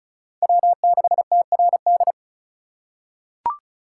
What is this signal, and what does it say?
Morse code, heard as a keyed tone, about 36 wpm, 700 Hz. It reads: W6TRD
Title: USA Call Signs - 5 Characters - 3x Character Spacing - 35wpm